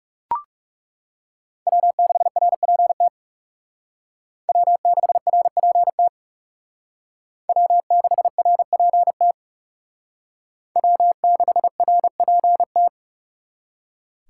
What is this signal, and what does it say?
Morse code, heard as a keyed tone, about 38 wpm, 700 Hz. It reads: W6RPT W6RPT W6RPT W6RPT